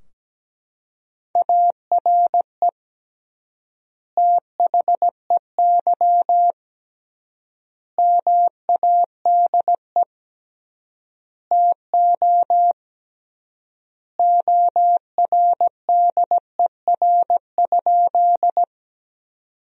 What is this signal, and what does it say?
Morse code, heard as a keyed tone, about 17 wpm, 700 Hz. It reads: ARE THEY MADE TO ORDER?
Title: 3x Word-Spacing - Sentences from Top 400 Words - 17wpm